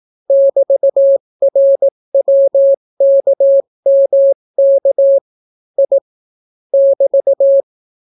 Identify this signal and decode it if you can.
Morse code, heard as a keyed tone, about 18 wpm, 550 Hz. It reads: <BT>RWKMK I <BT>